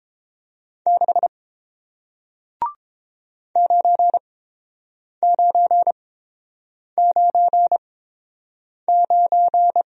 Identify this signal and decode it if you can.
Morse code, heard as a keyed tone, about 30 wpm, spaced as for 7 wpm, 700 Hz. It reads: 6 9999